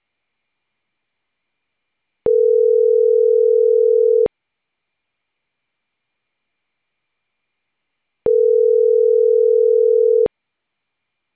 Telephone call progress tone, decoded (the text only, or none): ringback tone